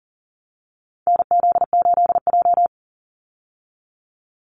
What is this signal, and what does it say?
Morse code, heard as a keyed tone, about 38 wpm, 700 Hz. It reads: NZ8J
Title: Call Signs - 40wpm